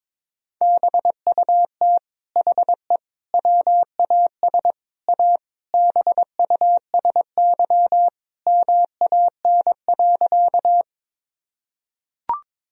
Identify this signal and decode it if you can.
Morse code, heard as a keyed tone, about 22 wpm, 700 Hz. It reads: BUT HE WAS A BUSY MAN.
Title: Sentences from Top 1000 Words - Review Part 3 - 22wpm